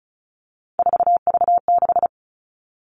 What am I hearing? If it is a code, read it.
Morse code, heard as a keyed tone, about 34 wpm, 700 Hz. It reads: 4V6
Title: All Cut Numbers - 3-Digit 35wpm